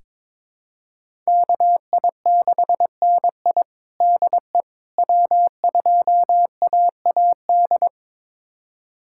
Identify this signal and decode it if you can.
Morse code, heard as a keyed tone, about 21 wpm, 700 Hz. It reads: KI6NI DE W2AAD
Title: USA Callsign DE Callsign - 5 Characters - 22wpm